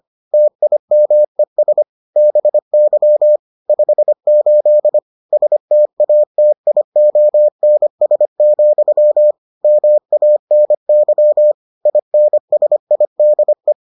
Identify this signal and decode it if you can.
Morse code, heard as a keyed tone, about 25 wpm, 600 Hz. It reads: TIMES BY 58 STATIONS, MANY INSIDE